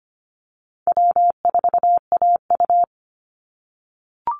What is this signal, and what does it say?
Morse code, heard as a keyed tone, about 25 wpm, 700 Hz. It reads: W4AU
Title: Call Signs - 25wpm